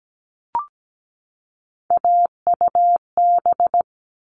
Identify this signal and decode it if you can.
Morse code, heard as a keyed tone, about 17 wpm, 700 Hz. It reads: AUB
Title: All Cut Numbers - 3-Digit 17wpm